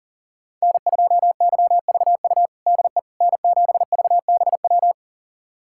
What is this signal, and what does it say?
Morse code, heard as a keyed tone, about 40 wpm, 700 Hz. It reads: N2YVU DE N7VBW
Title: USA Callsign DE Callsign 40wpm